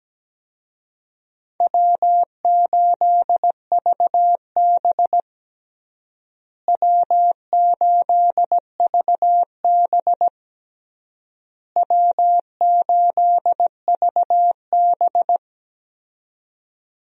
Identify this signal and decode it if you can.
Morse code, heard as a keyed tone, about 17 wpm, 700 Hz. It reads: W8VB W8VB W8VB
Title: USA Call Signs - 4 Characters - Repeated 3x - 17wpm